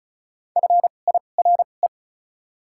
Morse code, heard as a keyed tone, about 34 wpm, spaced as for 25 wpm, 700 Hz. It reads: FIRE
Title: 2x Character Spacing - Top 500 Words - 35wpm